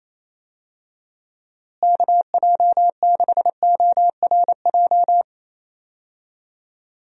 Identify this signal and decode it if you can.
Morse code, heard as a keyed tone, about 28 wpm, 700 Hz. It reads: KJ6ORJ
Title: USA Call Signs - 6 Characters - Repeated 2x - 28wpm